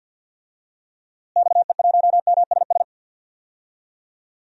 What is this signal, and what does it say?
Morse code, heard as a keyed tone, about 49 wpm, 700 Hz. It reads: XE1MSS
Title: DXCC-100 Call Signs - 6 Characters - Repeated 3x - 50wpm